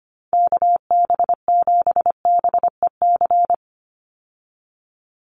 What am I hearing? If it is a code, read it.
Morse code, heard as a keyed tone, about 25 wpm, 700 Hz. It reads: KB7BEC
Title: USA Call Signs 25wpm